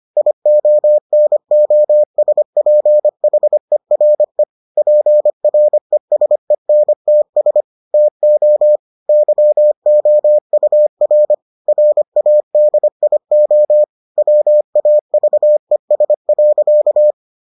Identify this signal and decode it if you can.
Morse code, heard as a keyed tone, about 25 wpm, 600 Hz. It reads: IONOSPHERE PRESENTS TO YOUR RADIO WAVES.